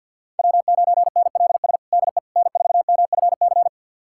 Morse code, heard as a keyed tone, about 51 wpm, 690 Hz. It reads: W9NLS DE N4MFC
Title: USA Callsign DE Callsign 50wpm